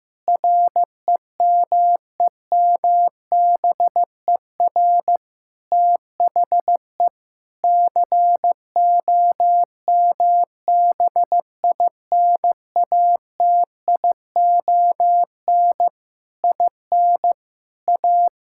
Morse code, heard as a keyed tone, about 15 wpm, 700 Hz. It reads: REMEMBER THE COMBINATION IN A
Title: Sentences from Top 1000 Words - Review Part 6 - 15wpm